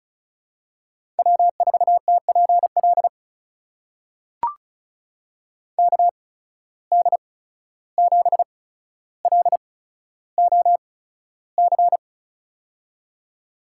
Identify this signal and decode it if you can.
Morse code, heard as a keyed tone, about 36 wpm, spaced as for 14 wpm, 700 Hz. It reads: W4TPL KD7LOC
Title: USA Call Signs - 8x Character Spacing - 35wpm